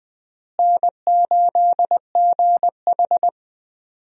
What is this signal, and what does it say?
Morse code, heard as a keyed tone, about 20 wpm, 700 Hz. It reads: N8GH
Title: USA Call Signs - 4 Characters - Repeated 3x - 20wpm